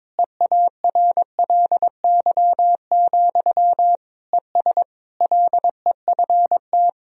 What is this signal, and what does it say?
Morse code, heard as a keyed tone, about 22 wpm, 700 Hz. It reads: EARLY, ES LEFT
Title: Sentences from Top 400 Words and Abbreviations 22wpm